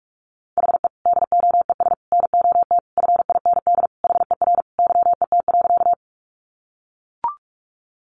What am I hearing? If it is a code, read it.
Morse code, heard as a keyed tone, about 44 wpm, 700 Hz. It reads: HE DOES NOT FIND HER YET.
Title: Easy Phrases 45wpm